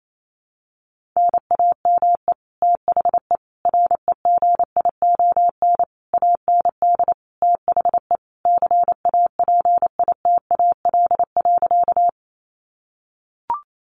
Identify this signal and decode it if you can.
Morse code, heard as a keyed tone, about 28 wpm, 700 Hz. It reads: NAME THE REGION AND THE CAPITAL.
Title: Sentences from Top 1500 Words 28wpm